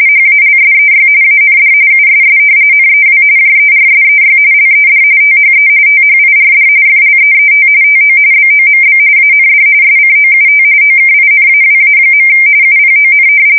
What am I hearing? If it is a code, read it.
RTTY, about 45.45 baud. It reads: HALSEY AND USS NEW JERSEY LED THE THIRD FLEET TO BOMB AND SCOUT OKINAWA. UPON
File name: fsk170 / battleshipnj